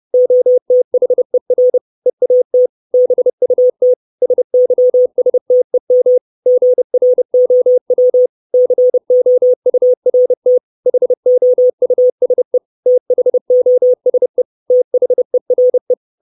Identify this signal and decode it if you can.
Morse code, heard as a keyed tone, about 30 wpm, 500 Hz. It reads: OTHER EAT BUT SYSTEM GROW COURT HOUSE THOSE THERE
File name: 632 Words 30WPM